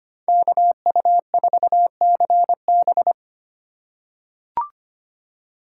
Morse code, heard as a keyed tone, about 25 wpm, 700 Hz. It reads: KU4CB
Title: USA Call Signs - 5 Characters - 2x Character Spacing - 25wpm